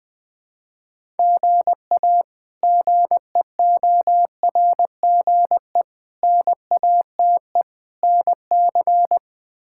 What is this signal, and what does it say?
Morse code, heard as a keyed tone, about 20 wpm, 700 Hz. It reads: GA GEORGE NATE NC
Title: K1USN SST QSO Practice - 20wpm